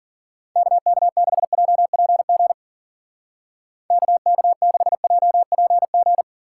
Morse code, heard as a keyed tone, about 50 wpm, 700 Hz. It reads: KK6JPG KK6JPG